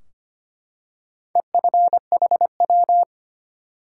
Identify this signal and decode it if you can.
Morse code, heard as a keyed tone, about 25 wpm, 700 Hz. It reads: EFHW
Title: Top 180 QSO Elements 25wpm